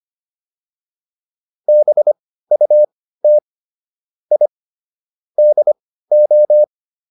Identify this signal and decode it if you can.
Morse code, heard as a keyed tone, about 25 wpm, spaced as for 15 wpm, 600 Hz. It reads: BUT I DO